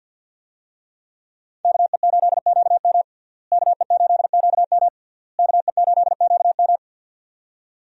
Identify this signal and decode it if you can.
Morse code, heard as a keyed tone, about 51 wpm, 700 Hz. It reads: KE8QM KE8QM KE8QM
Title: USA Call Signs - 5 Characters - Repeated 3x - 50wpm